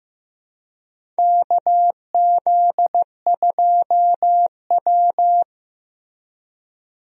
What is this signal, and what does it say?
Morse code, heard as a keyed tone, about 15 wpm, 700 Hz. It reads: KZ2W
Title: USA Call Signs - 4 Characters - Repeated 3x - 15wpm